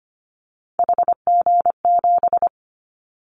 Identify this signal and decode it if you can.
Morse code, heard as a keyed tone, about 25 wpm, 700 Hz. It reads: HG7
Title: Callsign Prefixes for Top 50 DXCC - 25wpm